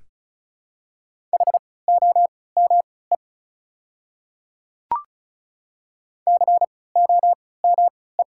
Morse code, heard as a keyed tone, about 36 wpm, spaced as for 20 wpm, 700 Hz. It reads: HOME COME
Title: 3x Character Spacing - Top 200 Words - 35wpm